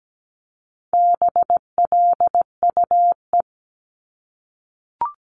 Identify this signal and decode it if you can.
Morse code, heard as a keyed tone, about 17 wpm, 700 Hz. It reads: BLUE
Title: Multi-letter Phonemes 17wpm